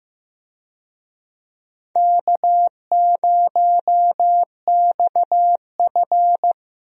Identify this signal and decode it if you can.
Morse code, heard as a keyed tone, about 15 wpm, 700 Hz. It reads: K0XF
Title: USA Call Signs - 4 Characters - Repeated 2x - 15wpm